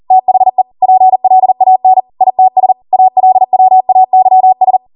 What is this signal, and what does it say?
Morse code, heard as a keyed tone, about 40 wpm, 750 Hz. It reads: THE PLAN ITS ALWAYS